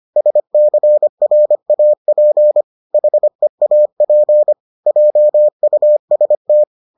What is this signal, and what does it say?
Morse code, heard as a keyed tone, about 25 wpm, 600 Hz. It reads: SCRAP HEAP JUST